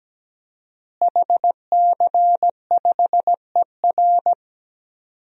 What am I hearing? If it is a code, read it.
Morse code, heard as a keyed tone, about 17 wpm, 700 Hz. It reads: HC5ER
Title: DXCC-100 Call Signs - 5 Characters - Repeated 2x - 17wpm